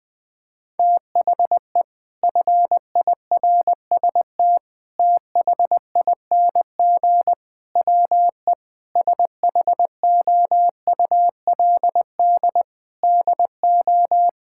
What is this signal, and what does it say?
Morse code, heard as a keyed tone, about 20 wpm, 700 Hz. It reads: THE FIRST THING WE SHOULD DO
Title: Sentences from Top 1000 Words - Review Part 6 - 20wpm